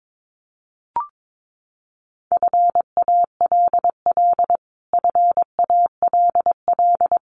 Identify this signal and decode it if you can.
Morse code, heard as a keyed tone, about 22 wpm, 700 Hz. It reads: FALL FALL
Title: Top 500 Words Repeated 22wpm